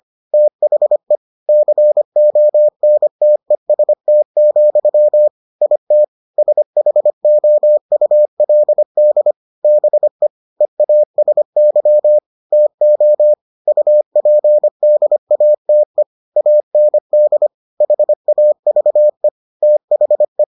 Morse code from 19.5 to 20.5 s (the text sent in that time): THE